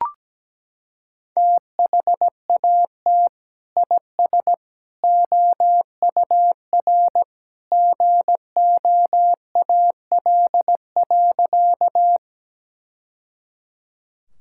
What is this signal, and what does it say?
Morse code, heard as a keyed tone, about 17 wpm, 700 Hz. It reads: THAT IS OUR GOAL.
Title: Sentences from Top 1000 Words - Review Part 5 - 17wpm